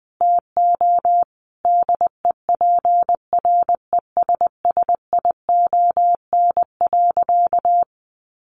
Morse code, heard as a keyed tone, about 20 wpm, 700 Hz. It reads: TO DEPRESSION.
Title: Sentences from Top 1300 Words 20wpm